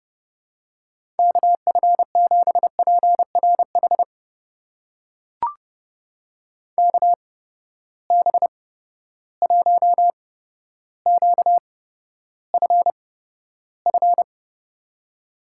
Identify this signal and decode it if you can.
Morse code, heard as a keyed tone, about 32 wpm, spaced as for 13 wpm, 700 Hz. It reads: KF7PRH KB1QFF
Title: USA Call Signs - 6 Characters - 8x Character Spacing - 30wpm